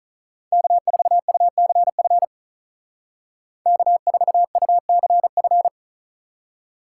Morse code, heard as a keyed tone, about 45 wpm, 700 Hz. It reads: K4UCF K4UCF